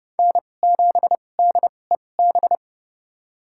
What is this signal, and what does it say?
Morse code, heard as a keyed tone, about 31 wpm, spaced as for 22 wpm, 700 Hz. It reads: N7DEB
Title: USA Call Signs - 5 Characters - 2x Character Spacing - 30wpm